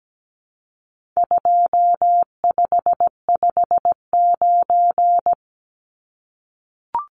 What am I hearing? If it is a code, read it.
Morse code, heard as a keyed tone, about 17 wpm, 700 Hz. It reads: 2559
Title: Numbers - 4-Digit 17wpm